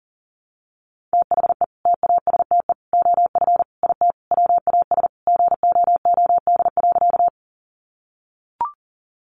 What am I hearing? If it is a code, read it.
Morse code, heard as a keyed tone, about 40 wpm, 700 Hz. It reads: THE TASTE OF IT WAS GOOD.
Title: Sentences from Top 1100 Words 40wpm